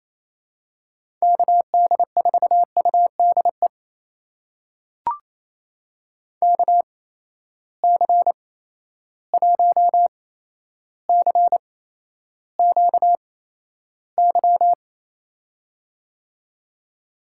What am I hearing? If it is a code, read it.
Morse code, heard as a keyed tone, about 28 wpm, spaced as for 12 wpm, 700 Hz. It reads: KD4UDE KC1CQY
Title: USA Call Signs - 6 Characters - 8x Character Spacing - 28wpm